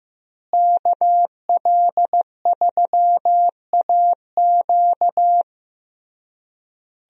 Morse code, heard as a keyed tone, about 15 wpm, 700 Hz. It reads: KL3AQ
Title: USA Call Signs - 5 Characters - Repeated 3x - 15wpm